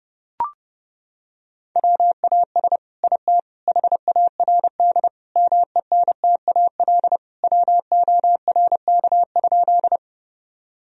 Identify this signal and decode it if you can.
Morse code, heard as a keyed tone, about 30 wpm, 700 Hz. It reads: WAS IT HARD MENTAL WORK?
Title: Sentences from Top 1000 Words - Review Part 4 - 30wpm